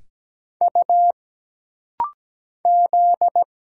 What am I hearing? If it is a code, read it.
Morse code, heard as a keyed tone, about 17 wpm, 700 Hz. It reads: U Z
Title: Single Letters - Rapid-Fire 17wpm